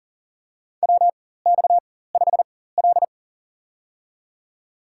Morse code, heard as a keyed tone, about 40 wpm, spaced as for 19 wpm, 700 Hz. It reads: WX5L
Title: USA Call Signs - 4x Character Spacing - 40wpm